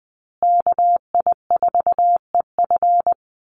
Morse code, heard as a keyed tone, about 20 wpm, 700 Hz. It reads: KI4EF
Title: USA Call Signs Repeated 2x 20wpm